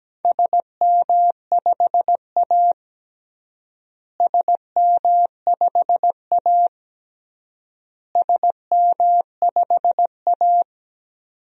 Morse code, heard as a keyed tone, about 17 wpm, 700 Hz. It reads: SM5A SM5A SM5A
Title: DXCC-100 Call Signs - 4 Characters - Repeated 3x - 17wpm